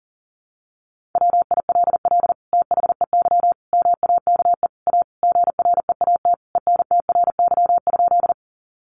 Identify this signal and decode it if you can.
Morse code, heard as a keyed tone, about 40 wpm, 700 Hz. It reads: WILL THEY MAKE A GREAT ENTRY?
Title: Sentences from Top 1500 Words 40wpm